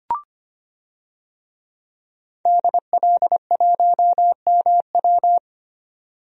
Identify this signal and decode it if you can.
Morse code, heard as a keyed tone, about 25 wpm, 700 Hz. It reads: DL1MW
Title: DXCC-100 Call Signs - 5 Characters - 25wpm